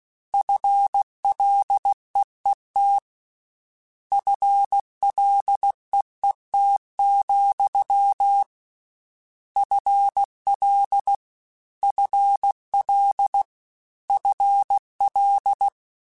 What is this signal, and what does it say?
Morse code, heard as a keyed tone, about 16 wpm, 800 Hz. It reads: FLEET FLEET, FL FL FL
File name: Combinations / Combinations-mp3s / Combinations@15